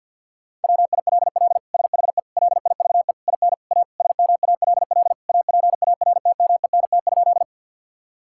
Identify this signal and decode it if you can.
Morse code, heard as a keyed tone, about 50 wpm, 700 Hz. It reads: WILL SHE LIVE IN A SMALL APARTMENT?
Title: Sentences from Top 1000 Words - Review Part 6 - 50wpm